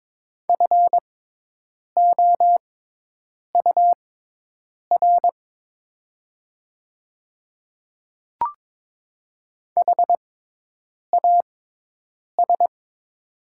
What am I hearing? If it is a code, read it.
Morse code, heard as a keyed tone, about 22 wpm, spaced as for 8 wpm, 700 Hz. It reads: FOUR HAS
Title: 6x Character Spacing - Top 300 Words - 22wpm